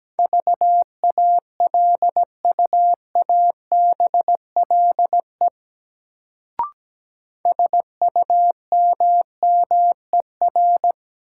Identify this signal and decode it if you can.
Morse code, heard as a keyed tone, about 17 wpm, 700 Hz. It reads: VALUABLE SUMMER
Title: Sentences from Top 1100 Words - New Word Drill Alt - 17wpm